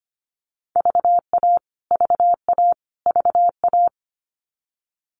Morse code, heard as a keyed tone, about 25 wpm, 700 Hz. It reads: VA VA VA
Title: US States Repeated 3x 25wpm